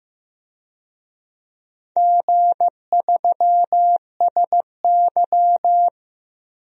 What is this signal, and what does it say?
Morse code, heard as a keyed tone, about 16 wpm, 700 Hz. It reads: G3SY
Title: DXCC-100 Call Signs - 4 Characters - Repeated 2x - 15wpm